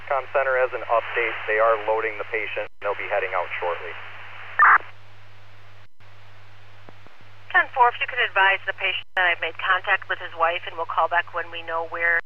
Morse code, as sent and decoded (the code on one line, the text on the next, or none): none